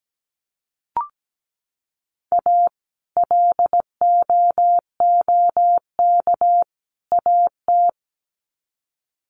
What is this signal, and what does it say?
Morse code, heard as a keyed tone, about 17 wpm, 700 Hz. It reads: A LOOK AT
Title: Sets of 3 Words 17wpm